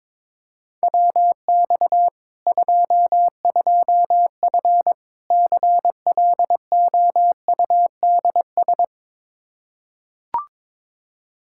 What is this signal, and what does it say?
Morse code, heard as a keyed tone, about 21 wpm, 700 Hz. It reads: WX 22F CLOUDS
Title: Practice QSO - Exchange 2 - Info - 22wpm